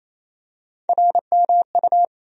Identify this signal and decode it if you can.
Morse code, heard as a keyed tone, about 28 wpm, 700 Hz. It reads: RMU
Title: USA Call Sign Postfixes - 3 Letters 28wpm